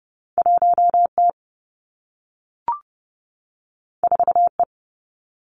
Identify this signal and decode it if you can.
Morse code, heard as a keyed tone, about 30 wpm, 700 Hz. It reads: 1 T 4 E
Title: Two Challenging Letter-Number Combinations 30wpm